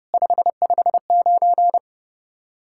Morse code, heard as a keyed tone, about 37 wpm, 700 Hz. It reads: 559
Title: Speed-Racing Common QSO Phrases Encore - 30wpm